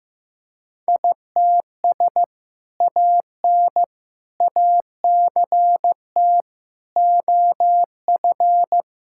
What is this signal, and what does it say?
Morse code, heard as a keyed tone, about 15 wpm, 700 Hz. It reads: ITS AN ACT OF
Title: Sentences from Top 1000 Words - Review Part 5 - 15wpm